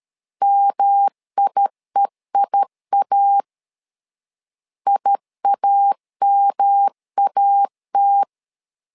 Morse code, heard as a keyed tone, about 13 wpm, 800 Hz. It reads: MIEIA IAMAT
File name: LEZIONI / 3 MWRO / 3 M 12wpm 1541